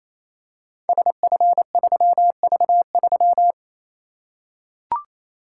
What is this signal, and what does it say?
Morse code, heard as a keyed tone, about 29 wpm, 700 Hz. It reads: SF3V3
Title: Code Groups - 5 Letter:Numbers - 28wpm